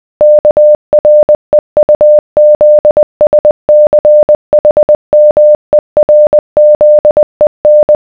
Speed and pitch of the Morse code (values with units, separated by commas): 20 wpm, 600 Hz